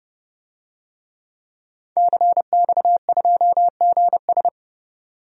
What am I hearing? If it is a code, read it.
Morse code, heard as a keyed tone, about 29 wpm, 700 Hz. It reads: CX2GS
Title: DXCC-100 Call Signs - 5 Characters - 30wpm